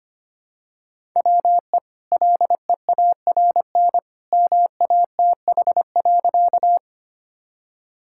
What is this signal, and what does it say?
Morse code, heard as a keyed tone, about 25 wpm, 700 Hz. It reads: WE LEARN MATH.
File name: Easy Sentences with Hints - 25wpm